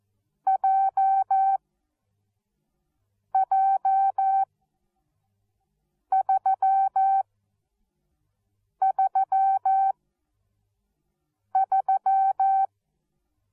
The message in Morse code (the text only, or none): J J 3 3 3